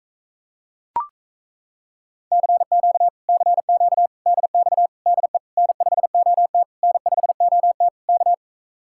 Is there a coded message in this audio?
Yes: Morse code from 2.3 to 8.3 s, heard as a keyed tone, about 37 wpm, 700 Hz. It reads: CQ CQ DX DE N5OT N5OT K